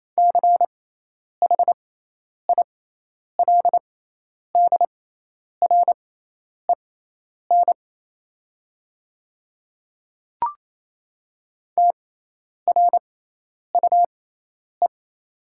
Morse code, heard as a keyed tone, about 27 wpm, spaced as for 10 wpm, 700 Hz. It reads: CHILDREN TRUE
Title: 6x Character Spacing - Top 500 Words - 28wpm